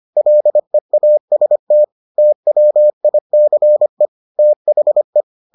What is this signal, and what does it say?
Morse code, heard as a keyed tone, about 25 wpm, 600 Hz. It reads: LEAST TWICE THE